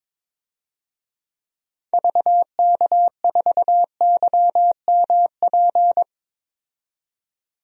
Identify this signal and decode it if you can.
Morse code, heard as a keyed tone, about 22 wpm, 700 Hz. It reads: VK4YMP
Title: DXCC-100 Call Signs - 6 Characters - 22wpm